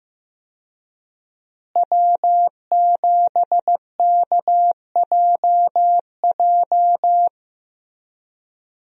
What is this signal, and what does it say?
Morse code, heard as a keyed tone, about 15 wpm, 700 Hz. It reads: W7KJJ
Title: USA Call Signs - 5 Characters - Repeated 2x - 15wpm